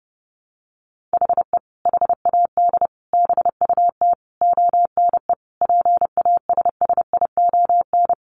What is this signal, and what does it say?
Morse code, heard as a keyed tone, about 30 wpm, 700 Hz. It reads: HE HAD BUT ONE PASSION
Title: Sentences from Top 1300 Words 30wpm